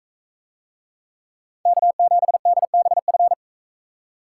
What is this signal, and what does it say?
Morse code, heard as a keyed tone, about 40 wpm, 700 Hz. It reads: K7DBF